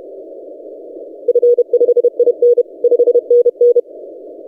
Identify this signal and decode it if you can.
Morse code, heard as a keyed tone, about 30 wpm, 480 Hz. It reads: F5IN 5NN
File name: posts / ri0x-20cw- -0519z